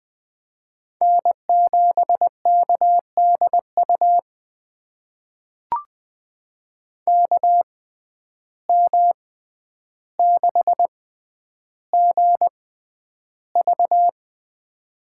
Morse code, heard as a keyed tone, about 20 wpm, spaced as for 10 wpm, 700 Hz. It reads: N7KDU KM6GV
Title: USA Call Signs - 5 Characters - 6x Character Spacing - 20wpm